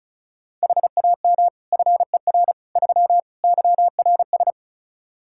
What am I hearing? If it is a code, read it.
Morse code, heard as a keyed tone, about 34 wpm, 700 Hz. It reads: HAM FER 3 YRS
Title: Practice QSO - Exchange 3 - More Info - 35wpm